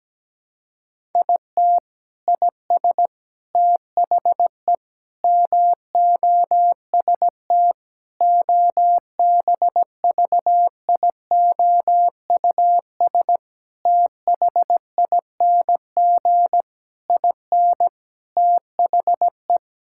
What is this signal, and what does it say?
Morse code, heard as a keyed tone, about 17 wpm, 700 Hz. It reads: IT IS THE MOST OBVIOUS THING IN THE
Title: Sentences from Top 1600 Words 17wpm